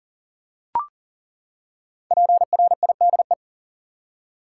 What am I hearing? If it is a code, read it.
Morse code, heard as a keyed tone, about 40 wpm, 700 Hz. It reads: PRIDE
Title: Five Letter Words - 40wpm